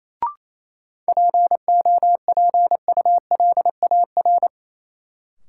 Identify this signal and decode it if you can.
Morse code, heard as a keyed tone, about 28 wpm, 700 Hz. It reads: POPULAR
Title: Sentences from Top 1000 Words - Review Part 3 - New Word Drill - 28wpm